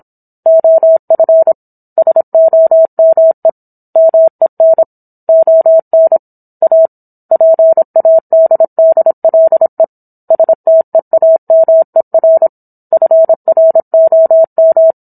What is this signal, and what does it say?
Morse code, heard as a keyed tone, about 27 wpm, 650 Hz. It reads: OF SOME MEN ON A PADDLE STEAMER FROM